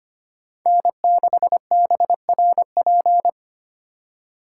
Morse code, heard as a keyed tone, about 33 wpm, 700 Hz. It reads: N6BRP